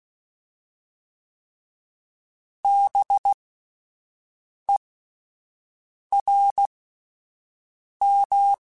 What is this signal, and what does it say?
Morse code, heard as a keyed tone, about 16 wpm, spaced as for 5 wpm, 800 Hz. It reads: BERM